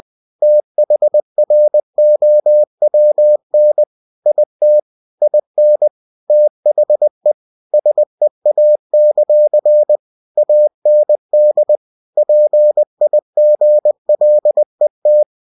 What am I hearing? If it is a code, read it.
Morse code, heard as a keyed tone, about 20 wpm, 600 Hz. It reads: THROWN IT IN THE SEA; AND PIGLET
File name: cw-books / winnie-the-pooh / winnie-20wpm-page-0092